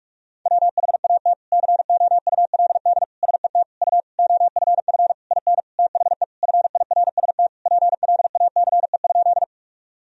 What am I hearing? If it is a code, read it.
Morse code, heard as a keyed tone, about 46 wpm, 700 Hz. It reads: WHAT COULD SET U OFF IN THE FIRST PLACE?
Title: Sentences from Top 200 Words and Abbreviations 45wpm